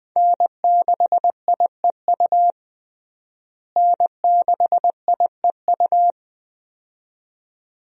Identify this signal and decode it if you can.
Morse code, heard as a keyed tone, about 20 wpm, 700 Hz. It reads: N6IEU N6IEU
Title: USA Call Signs - 5 Characters - Repeated 2x - 20wpm